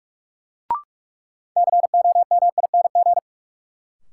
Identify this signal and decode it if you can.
Morse code, heard as a keyed tone, about 45 wpm, 700 Hz. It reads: COMING
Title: Sentences from Top 1300 Words - New Word Drill Alt - 45wpm